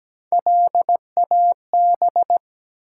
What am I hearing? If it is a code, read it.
Morse code, heard as a keyed tone, about 17 wpm, 700 Hz. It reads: LAB
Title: Three Letter Words - 17wpm